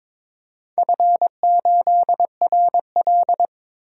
Morse code, heard as a keyed tone, about 23 wpm, 700 Hz. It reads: F8RL
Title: DXCC-100 Call Signs - 4 Characters - 22wpm